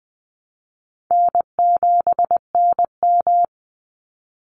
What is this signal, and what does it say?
Morse code, heard as a keyed tone, about 20 wpm, 700 Hz. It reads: N7NM
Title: Easy Call Signs - 20wpm